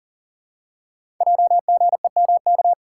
Morse code, heard as a keyed tone, about 40 wpm, 690 Hz. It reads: JGEMK
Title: Code Groups - 5 Letters - 40wpm